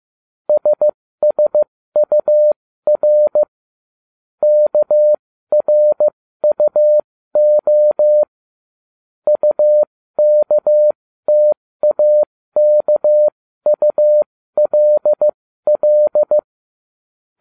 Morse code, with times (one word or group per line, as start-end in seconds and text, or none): 0.5-3.4 SSUR
4.4-8.2 KRUO
9.3-16.4 UKTAKULL